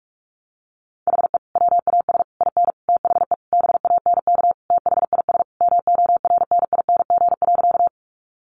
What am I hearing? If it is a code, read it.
Morse code, heard as a keyed tone, about 44 wpm, 700 Hz. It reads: HE WAS IN THE BANK THIS MORNING.
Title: Sentences from Top 900 Words 45wpm